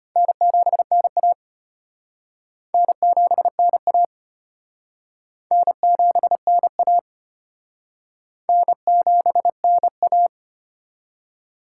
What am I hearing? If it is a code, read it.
Morse code, heard as a keyed tone, about 32 wpm, 700 Hz. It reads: N7NA N7NA N7NA N7NA